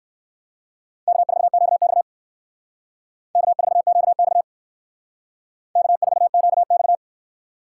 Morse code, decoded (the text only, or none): K4QX K4QX K4QX